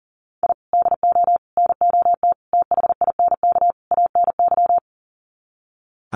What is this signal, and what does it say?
Morse code, heard as a keyed tone, about 40 wpm, 700 Hz. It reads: I DO NOT THINK ANY
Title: Sets of 5 Words Encore 40wpm